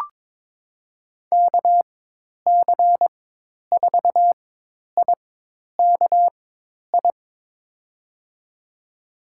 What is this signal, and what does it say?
Morse code, heard as a keyed tone, about 22 wpm, spaced as for 10 wpm, 700 Hz. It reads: KC4IKI
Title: USA Call Signs - 6 Characters - 4x Character Spacing - 22wpm